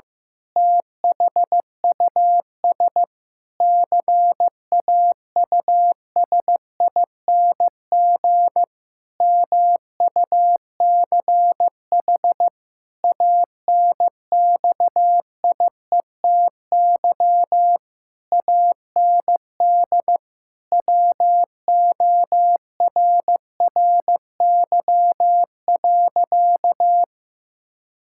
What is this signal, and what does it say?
Morse code, heard as a keyed tone, about 15 wpm, 700 Hz. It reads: THUS CAUSING MUCH ANXIETY AND WORRY.
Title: Pangram Challenge v2 - 15wpm